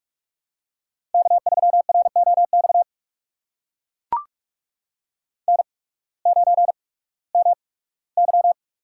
Morse code, heard as a keyed tone, about 44 wpm, spaced as for 21 wpm, 700 Hz. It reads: K3ROX N9MY
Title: USA Call Signs - 5 Characters - 8x Character Spacing - 45wpm